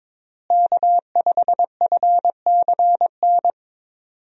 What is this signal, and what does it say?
Morse code, heard as a keyed tone, about 22 wpm, 700 Hz. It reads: K5FCN
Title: USA Call Signs - 5 Characters - Repeated 3x - 22wpm